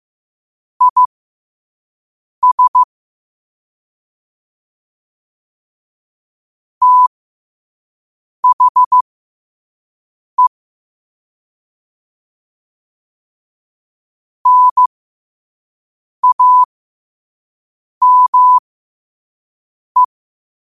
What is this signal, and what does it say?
Morse code, heard as a keyed tone, about 15 wpm, spaced as for 5 wpm, 1000 Hz. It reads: IS THE NAME
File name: qotdfiles / 5wpm